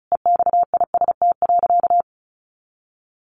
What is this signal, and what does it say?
Morse code, heard as a keyed tone, about 34 wpm, 700 Hz. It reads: EXIST.
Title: Sentences from Top 1500 Words 35wpm